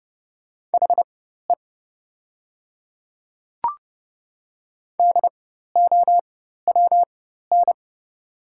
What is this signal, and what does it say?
Morse code, heard as a keyed tone, about 29 wpm, spaced as for 14 wpm, 700 Hz. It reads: HE DOWN
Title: 4x Character Spacing - Top 100 Words - 30wpm